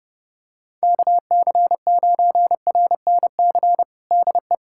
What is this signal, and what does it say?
Morse code, heard as a keyed tone, about 31 wpm, 700 Hz. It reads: KC9RNC DE
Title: USA Callsign DE Callsign 30wpm